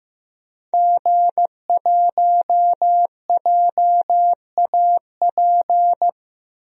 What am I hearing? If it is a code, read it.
Morse code, heard as a keyed tone, about 15 wpm, 700 Hz. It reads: G1JAP
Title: DXCC-100 Call Signs - 5 Characters - 15wpm